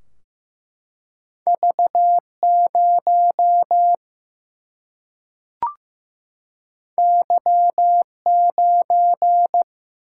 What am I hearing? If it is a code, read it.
Morse code, heard as a keyed tone, about 15 wpm, 700 Hz. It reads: V 0 Y 9
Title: Single-Letter Single-Number 15wpm